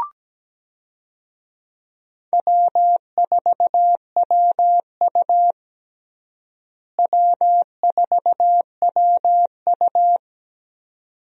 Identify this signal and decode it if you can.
Morse code, heard as a keyed tone, about 17 wpm, 700 Hz. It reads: W4WU W4WU
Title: USA Call Signs - 4 Characters - Repeated 2x - 17wpm